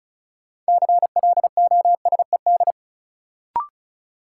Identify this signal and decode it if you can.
Morse code, heard as a keyed tone, about 36 wpm, 700 Hz. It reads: CLOSED
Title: Sentences from Top 1400 Words - New Word Drill Alt - 35wpm